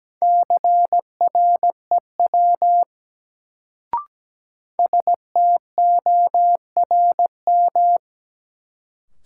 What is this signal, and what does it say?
Morse code, heard as a keyed tone, about 17 wpm, 700 Hz. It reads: CREW STORM
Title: Sentences from Top 1500 Words - New Word Drill - 17wpm